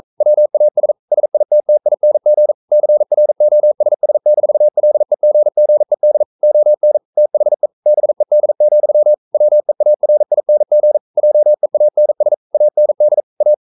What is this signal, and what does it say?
Morse code, heard as a keyed tone, about 42 wpm, 600 Hz. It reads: WAS SITTING CROSS-LEGGED ON THE BED, WEARING JEANS AND A
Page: Romanen Never af Ken Follet på engelsk.